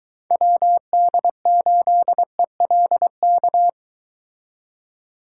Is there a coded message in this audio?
Yes: Morse code from 0.3 to 3.7 s, heard as a keyed tone, about 23 wpm, 700 Hz. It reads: WD8ELK